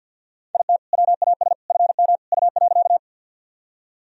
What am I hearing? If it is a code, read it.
Morse code, heard as a keyed tone, about 47 wpm, 700 Hz. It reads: IT WAS FM U.
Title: Sentences from Top 100 Words and Abbreviations 50wpm